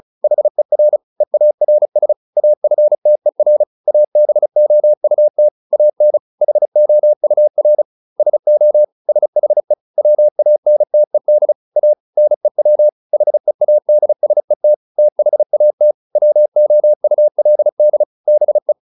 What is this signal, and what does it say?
Morse code, heard as a keyed tone, about 34 wpm, 600 Hz. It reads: HER EARS AFTER ABOUT AN HOUR SO SHE WANTED A NEW HEADSET THAT WOULD BE
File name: SS303 35